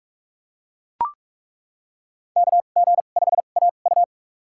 Morse code, heard as a keyed tone, about 42 wpm, spaced as for 31 wpm, 700 Hz. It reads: KG5AU
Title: USA Call Signs - 5 Characters - 2x Character Spacing - 45wpm